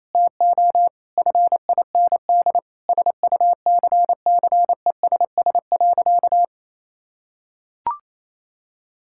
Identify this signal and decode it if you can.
Morse code, heard as a keyed tone, about 28 wpm, 700 Hz. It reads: TO FIND SUCCESS.
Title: Sentences from Top 1000 Words - Review Part 5 - 28wpm